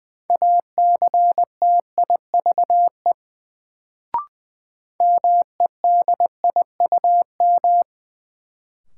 Morse code, heard as a keyed tone, about 20 wpm, 700 Hz. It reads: ACTIVE MEDIUM
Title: Sentences from Top 900 Words - New Word Drill - 20wpm